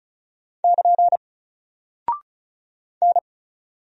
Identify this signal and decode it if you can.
Morse code, heard as a keyed tone, about 36 wpm, 700 Hz. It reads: <KN> N
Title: Single Character (Letter:Number:Punctuation:Prosign) Alt 35wpm